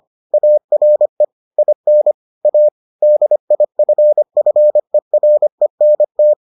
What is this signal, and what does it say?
Morse code, heard as a keyed tone, about 25 wpm, 600 Hz. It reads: ARE IN A DIFFERENT